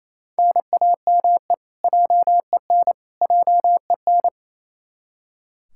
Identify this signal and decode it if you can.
Morse code, heard as a keyed tone, about 28 wpm, 700 Hz. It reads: NAME JEN JEN
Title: K1USN SST QSO Practice 28wpm